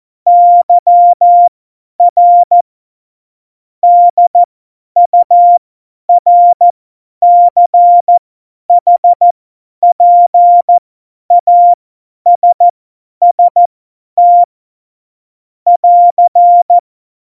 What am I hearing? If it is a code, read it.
Morse code, heard as a keyed tone, about 14 wpm, spaced as for 10 wpm, 700 Hz. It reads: YR DURCHPASST <AR>